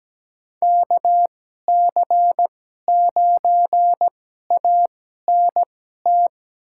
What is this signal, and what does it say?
Morse code, heard as a keyed tone, about 17 wpm, spaced as for 12 wpm, 700 Hz. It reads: KC9ANT
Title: USA Call Signs - 6 Characters - 2x Character Spacing - 17wpm